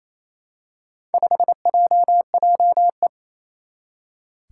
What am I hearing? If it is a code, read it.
Morse code, heard as a keyed tone, about 28 wpm, 700 Hz. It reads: 5JJE
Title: Four Challenging Letter-Number Combinations 28wpm